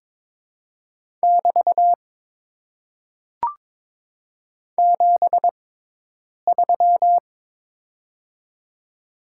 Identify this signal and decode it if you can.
Morse code, heard as a keyed tone, about 22 wpm, spaced as for 8 wpm, 700 Hz. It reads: <BT> 73
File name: Top 90 QSO Elements - 6x Character Spacing - 22wpm